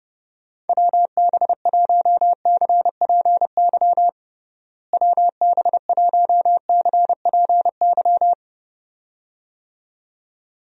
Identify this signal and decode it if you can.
Morse code, heard as a keyed tone, about 30 wpm, 700 Hz. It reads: WB1CPY WB1CPY
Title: USA Call Signs - 6 Characters - Repeated 2x - 30wpm